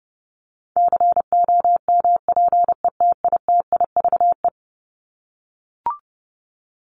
Morse code, heard as a keyed tone, about 30 wpm, 700 Hz. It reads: COMPETITIVE
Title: Top 1001-1250 Words 30wpm